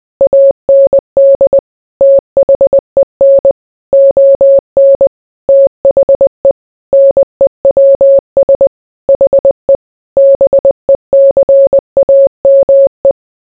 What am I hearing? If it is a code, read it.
Morse code, heard as a keyed tone, about 20 wpm, 550 Hz. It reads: AND THEN ON THE NEWS HE BECAME